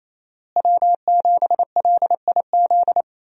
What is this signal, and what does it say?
Morse code, heard as a keyed tone, about 28 wpm, 700 Hz. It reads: W7LIZ
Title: USA Call Signs - 5 Characters - Repeated 2x - 28wpm